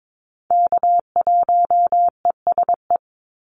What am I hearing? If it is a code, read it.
Morse code, heard as a keyed tone, about 22 wpm, 700 Hz. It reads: K1ESE
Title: Call Signs - 22wpm